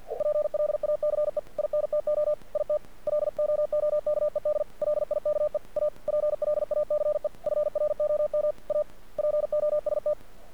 Morse code, heard as a keyed tone, about 50 wpm, 600 Hz. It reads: PLACE INTO IT LOOKED LIKE A PLACE FROM A POST
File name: archived / CRSnet